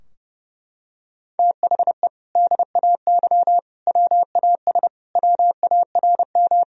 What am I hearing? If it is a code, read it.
Morse code, heard as a keyed tone, about 30 wpm, 700 Hz. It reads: THE DAY WAS WARM
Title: Sentences from Top 1000 Words - Review Part 2 - 30wpm